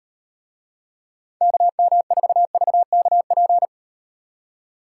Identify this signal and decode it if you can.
Morse code, heard as a keyed tone, about 36 wpm, 700 Hz. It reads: KM4VKP